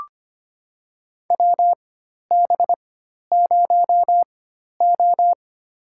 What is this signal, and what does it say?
Morse code, heard as a keyed tone, about 25 wpm, spaced as for 12 wpm, 700 Hz. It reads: WB0O
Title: USA Call Signs - 4 Characters - 4x Character Spacing - 25wpm